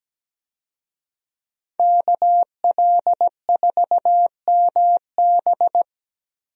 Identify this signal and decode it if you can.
Morse code, heard as a keyed tone, about 17 wpm, 700 Hz. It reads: KL4MB
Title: USA Call Signs - 5 Characters - Repeated 2x - 17wpm